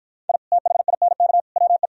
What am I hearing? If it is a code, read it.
Morse code, heard as a keyed tone, about 52 wpm, 700 Hz. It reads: I THINK WE